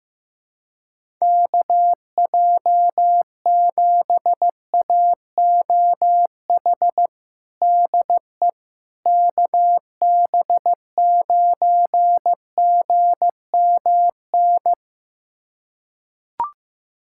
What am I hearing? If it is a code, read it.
Morse code, heard as a keyed tone, about 15 wpm, 700 Hz. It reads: KJ7AOH DE KB9GMN
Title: USA Callsign DE Callsign - 6 Characters - 15wpm